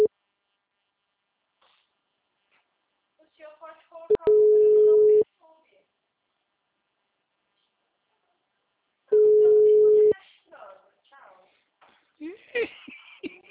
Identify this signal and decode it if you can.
Ringback tone (Europe, 425 Hz), with speech.
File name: out- -2034- - - .58006